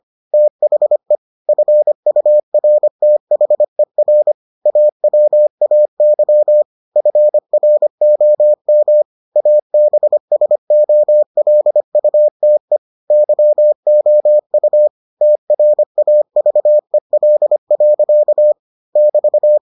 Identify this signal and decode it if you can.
Morse code, heard as a keyed tone, about 25 wpm, 600 Hz. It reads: THE FURTHER AWAY FROM ABSOLUTE YOU TRAVEL. <BT>